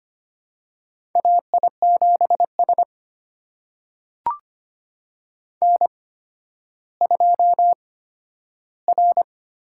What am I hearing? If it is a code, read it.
Morse code, heard as a keyed tone, about 25 wpm, spaced as for 12 wpm, 700 Hz. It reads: AI7S N2R
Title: USA Call Signs - 8x Character Spacing - 25wpm